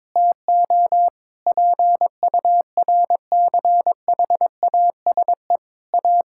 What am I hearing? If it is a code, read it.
Morse code, heard as a keyed tone, about 22 wpm, 700 Hz. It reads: TO PURCHASE A
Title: Sentences from Top 1000 Words - Review Part 5 - 22wpm